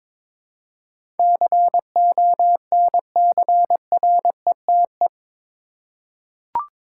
Morse code, heard as a keyed tone, about 22 wpm, 700 Hz. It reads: CONCRETE
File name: practice / 22wpm / Eight Letter Words - 22wpm